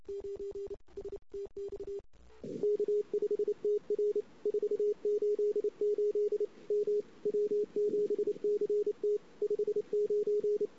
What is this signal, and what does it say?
Morse code, heard as a keyed tone, about 29 wpm, 400 Hz. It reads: 9 STX K5TR 488 M W7CT 59
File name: CW / 2008-ss-cw-k5tr-op-n5rz -06:00